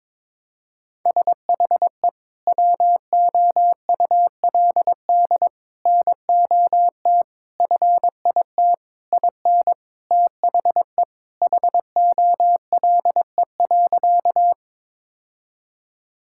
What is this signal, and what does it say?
Morse code, heard as a keyed tone, about 21 wpm, 700 Hz. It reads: SHE WOULD NOT FIT IN THE HOLE.
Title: Sentences from Top 1600 Words 22wpm